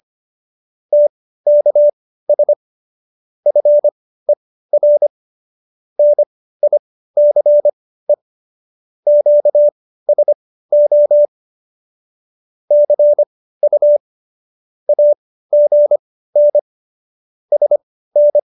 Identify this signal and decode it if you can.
Morse code, heard as a keyed tone, about 25 wpm, spaced as for 15 wpm, 590 Hz. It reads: TKS FER NICE QSO CU AGN SN
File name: qso222 15